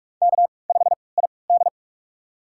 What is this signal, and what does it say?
Morse code, heard as a keyed tone, about 44 wpm, spaced as for 25 wpm, 690 Hz. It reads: K5ID
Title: USA Call Signs - 4 Characters - 3x Character Spacing - 45wpm